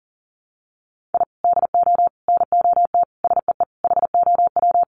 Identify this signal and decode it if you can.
Morse code, heard as a keyed tone, about 40 wpm, 700 Hz. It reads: I DO NOT SEE HOW
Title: Sets of 5 Words 40wpm